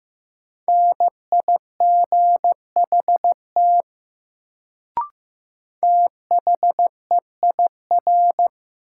Morse code, heard as a keyed tone, about 15 wpm, 700 Hz. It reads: NIGHT THEIR
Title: Sentences from Top 300 Words - New Word Drill - 15wpm